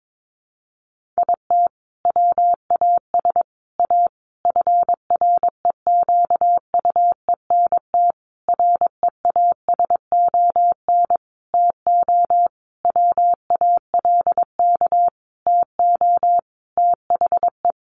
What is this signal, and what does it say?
Morse code, heard as a keyed tone, about 22 wpm, 700 Hz. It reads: IT WAS A FREQUENT REASON TO WALK TO THE
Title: Sentences from Top 1400 Words 22wpm